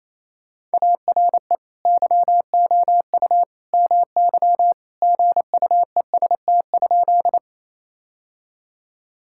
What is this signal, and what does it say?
Morse code, heard as a keyed tone, about 28 wpm, 700 Hz. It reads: ARE YOU MY GUEST?
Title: Sentences from Top 1600 Words 28wpm